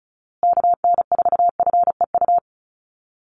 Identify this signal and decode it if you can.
Morse code, heard as a keyed tone, about 34 wpm, 700 Hz. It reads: KN4FEU
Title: USA Call Signs Repeated 2x 35wpm